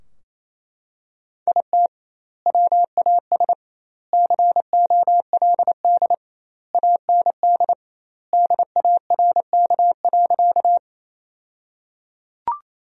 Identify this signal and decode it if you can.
Morse code, heard as a keyed tone, about 28 wpm, 700 Hz. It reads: IT WAS COLD AND DARK.
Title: 2x Word-Spacing - Sentences from Top 500 Words - 28wpm